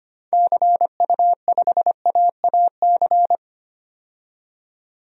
Morse code, heard as a keyed tone, about 25 wpm, 700 Hz. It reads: CU5AAC
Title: DXCC-100 Call Signs - 6 Characters - 25wpm